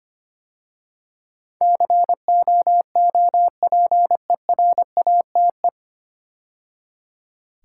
Morse code, heard as a keyed tone, about 25 wpm, 690 Hz. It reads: COOPERATE
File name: practice / 25wpm / 9 Letter Words - First 3 Letters Given - 25wpm